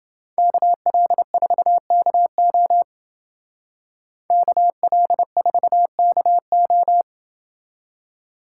Morse code, heard as a keyed tone, about 30 wpm, 700 Hz. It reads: KL4KO KL4KO